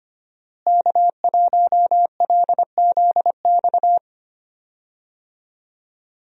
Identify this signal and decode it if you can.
Morse code, heard as a keyed tone, about 25 wpm, 700 Hz. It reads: K1LZX
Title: USA Call Signs - 5 Characters - Repeated 2x - 25wpm